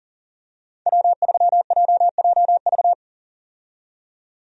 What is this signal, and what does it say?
Morse code, heard as a keyed tone, about 40 wpm, 700 Hz. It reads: W3JJV
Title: USA Call Signs - 5 Characters - 40wpm